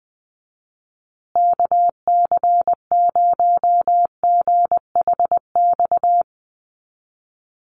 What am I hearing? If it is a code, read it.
Morse code, heard as a keyed tone, about 20 wpm, 700 Hz. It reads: KC0GHX
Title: USA Call Signs Repeated 3x 20wpm